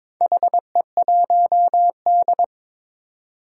Morse code, heard as a keyed tone, about 22 wpm, 700 Hz. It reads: HE1D
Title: Four Challenging Letter-Number Combinations 22wpm